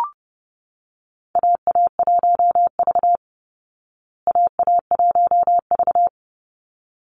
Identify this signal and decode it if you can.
Morse code, heard as a keyed tone, about 30 wpm, 700 Hz. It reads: AA1V AA1V
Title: Call Signs Repeated 3x - 30wpm